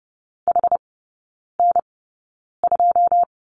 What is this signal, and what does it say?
Morse code, heard as a keyed tone, about 30 wpm, spaced as for 9 wpm, 700 Hz. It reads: HN2
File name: 081 Lesson 14e - Number 2 - Three Characters 30wpm